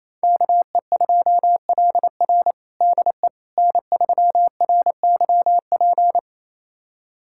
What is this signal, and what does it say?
Morse code, heard as a keyed tone, about 28 wpm, 700 Hz. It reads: KE2LR DE N3RYP
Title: USA Callsign DE Callsign - 5 Characters - 28wpm